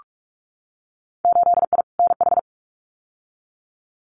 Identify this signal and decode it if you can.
Morse code, heard as a keyed tone, about 46 wpm, 690 Hz. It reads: 8I NH
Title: Winter Field Day Exchanges Encore 45wpm